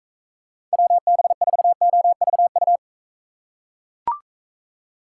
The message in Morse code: WB4OVU

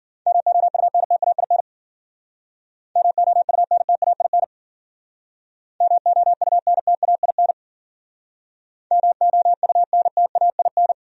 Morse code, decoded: MOUNTAIN MOUNTAIN MOUNTAIN MOUNTAIN

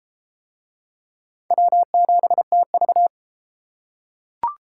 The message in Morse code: W7TV

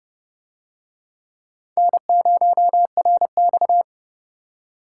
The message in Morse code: N0RX

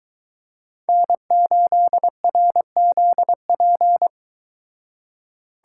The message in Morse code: N8RZP

